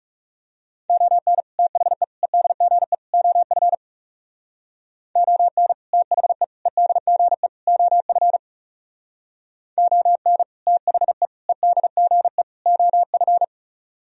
Morse code, heard as a keyed tone, about 35 wpm, 700 Hz. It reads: ON THE EDGE OF ON THE EDGE OF ON THE EDGE OF
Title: Speed-Racing - Sets of 4 Words 30wpm